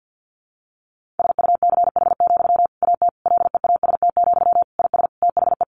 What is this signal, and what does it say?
Morse code, heard as a keyed tone, about 50 wpm, 700 Hz. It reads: SUCH, AT LEAST, IS THE
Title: Sentences from Top 1500 Words 50wpm